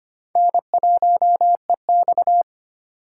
Morse code, heard as a keyed tone, about 32 wpm, 700 Hz. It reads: N1EX